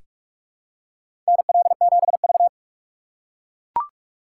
Morse code, heard as a keyed tone, about 43 wpm, 700 Hz. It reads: NL7V
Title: K1USN SST QSO Practice - 45wpm